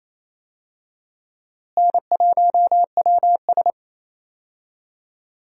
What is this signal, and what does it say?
Morse code, heard as a keyed tone, about 28 wpm, 700 Hz. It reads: N1WS
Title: USA Call Signs - 4 Characters - Repeated 3x - 28wpm